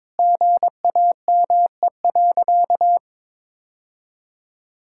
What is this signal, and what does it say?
Morse code, heard as a keyed tone, about 22 wpm, 700 Hz. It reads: GAME.
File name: practice / 22wpm / Sentences with 3-Word Phrases - 22wpm